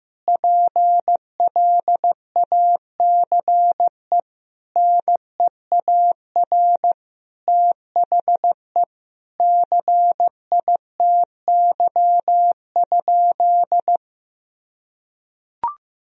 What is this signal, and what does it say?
Morse code, heard as a keyed tone, about 15 wpm, 700 Hz. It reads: PLACE NEAR THE CITY?
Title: Sentences from Top 1000 Words - Review Part 3 - 15wpm